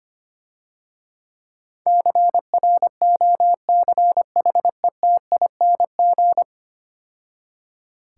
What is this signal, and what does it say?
Morse code, heard as a keyed tone, about 25 wpm, 700 Hz. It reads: CROCHETING
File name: Easy Words - Speed-Racing with Hints - 25wpm